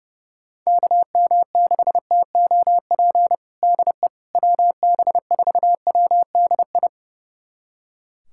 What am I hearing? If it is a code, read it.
Morse code, heard as a keyed tone, about 30 wpm, 700 Hz. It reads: KM6TOP DE WB4WDI
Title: USA Callsign DE Callsign - 6 Characters - 30wpm